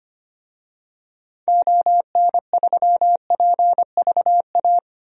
Morse code, heard as a keyed tone, about 25 wpm, 700 Hz. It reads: ON3PVA
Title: DXCC-100 Call Signs - 6 Characters - Repeated 2x - 25wpm